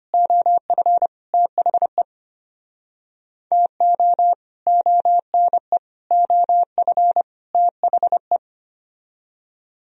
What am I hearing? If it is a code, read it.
Morse code, heard as a keyed tone, about 32 wpm, 700 Hz. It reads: OF THE TO ONE OF THE